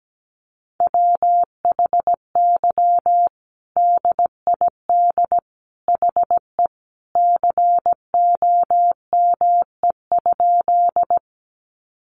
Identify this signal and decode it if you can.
Morse code, heard as a keyed tone, about 17 wpm, 700 Hz. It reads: WHY DID HE COME?
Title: Sentences from Top 200 Words 17wpm